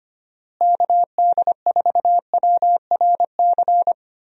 Morse code, heard as a keyed tone, about 25 wpm, 700 Hz. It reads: KD4WRC
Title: USA Call Signs - 6 Characters - Repeated 3x - 25wpm